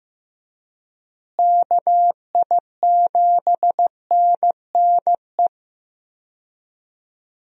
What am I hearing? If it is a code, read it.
Morse code, heard as a keyed tone, about 15 wpm, 700 Hz. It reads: KI7NNE
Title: USA Call Signs - 6 Characters - Repeated 2x - 15wpm